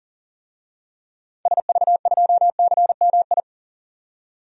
Morse code, heard as a keyed tone, about 40 wpm, 700 Hz. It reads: SV2CMI
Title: DXCC-100 Call Signs - 6 Characters - 40wpm